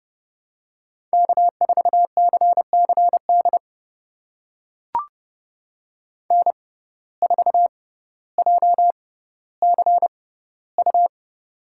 Morse code, heard as a keyed tone, about 30 wpm, spaced as for 15 wpm, 700 Hz. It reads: K4CCD N4JCU
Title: USA Call Signs - 5 Characters - 6x Character Spacing - 30wpm